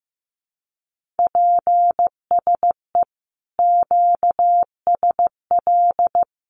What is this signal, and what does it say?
Morse code, heard as a keyed tone, about 15 wpm, 700 Hz. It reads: PSE QSL
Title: Common QSO Phrases 15wpm